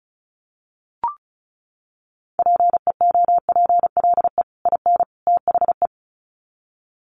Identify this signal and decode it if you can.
Morse code, heard as a keyed tone, about 34 wpm, 700 Hz. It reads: PEOPLE IN THE
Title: Sets of 3 Words 35wpm